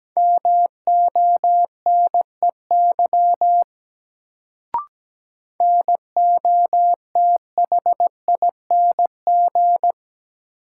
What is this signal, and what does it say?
Morse code, heard as a keyed tone, about 17 wpm, 700 Hz. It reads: MONEY NOTHING
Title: Sets of 4 Words - New Word Drill - 17wpm